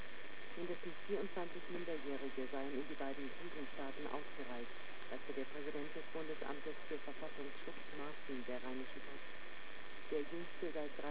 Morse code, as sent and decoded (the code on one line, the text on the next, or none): none